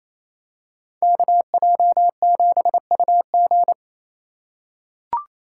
Morse code, heard as a keyed tone, about 28 wpm, 700 Hz. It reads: KJ7UG
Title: USA Call Signs - 3x Character Spacing - 28wpm